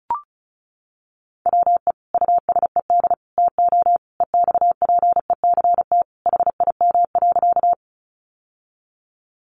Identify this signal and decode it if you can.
Morse code, heard as a keyed tone, about 34 wpm, 700 Hz. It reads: WE USED TO EXPECT HIM.
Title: Sentences from Top 1100 Words 35wpm